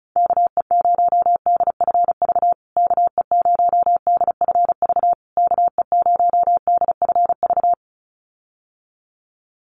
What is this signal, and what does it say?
Morse code, heard as a keyed tone, about 34 wpm, 700 Hz. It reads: KE0DFV KE0DFV KE0DFV
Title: USA Call Signs Repeated 3x 35wpm